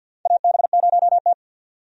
Morse code, heard as a keyed tone, about 46 wpm, 700 Hz. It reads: AB0T
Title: USA Call Signs - 4 Characters - 50wpm